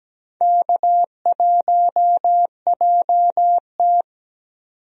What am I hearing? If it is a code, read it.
Morse code, heard as a keyed tone, about 17 wpm, 700 Hz. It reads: K1JT
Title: USA Call Signs - 4 Characters - 17wpm